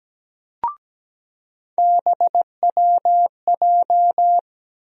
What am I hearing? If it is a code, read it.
Morse code, heard as a keyed tone, about 17 wpm, 700 Hz. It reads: BWJ
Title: USA Call Sign Postfixes - 3 Letters 17wpm